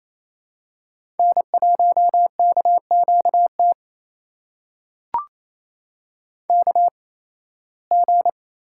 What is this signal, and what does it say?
Morse code, heard as a keyed tone, about 28 wpm, spaced as for 18 wpm, 700 Hz. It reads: N1KQT KG
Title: USA Call Signs - 8x Character Spacing - 28wpm